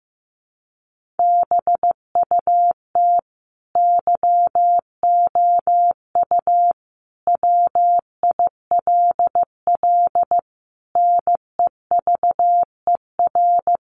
Morse code, heard as a keyed tone, about 15 wpm, 700 Hz. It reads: BUT YOU WILL NEVER
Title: Sentences from Top 1500 Words 15wpm